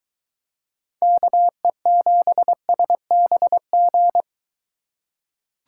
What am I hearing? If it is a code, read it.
Morse code, heard as a keyed tone, about 23 wpm, 700 Hz. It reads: KE7SBG